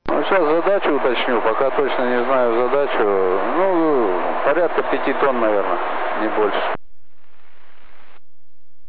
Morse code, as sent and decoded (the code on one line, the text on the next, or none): none